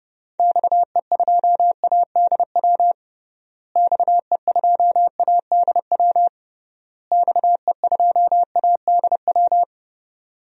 Morse code, heard as a keyed tone, about 30 wpm, 700 Hz. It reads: XE2ADW XE2ADW XE2ADW
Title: DXCC-100 Call Signs - 6 Characters - Repeated 3x - 30wpm